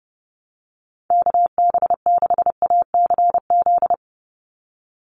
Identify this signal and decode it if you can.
Morse code, heard as a keyed tone, about 30 wpm, 700 Hz. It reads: KB6ACZ
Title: USA Call Signs 30wpm